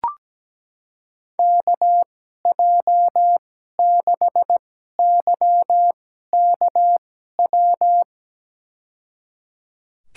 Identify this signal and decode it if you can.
Morse code, heard as a keyed tone, about 17 wpm, spaced as for 12 wpm, 700 Hz. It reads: KJ6YKW
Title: USA Call Signs - 2x Character Spacing - 17wpm